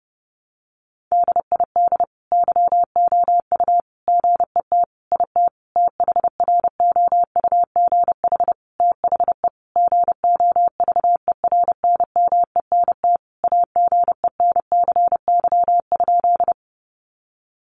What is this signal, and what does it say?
Morse code, heard as a keyed tone, about 30 wpm, 700 Hz. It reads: DID YOU GET IT THROUGH THE GOVERNMENT AGENCY?
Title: Sentences from Top 1300 Words 30wpm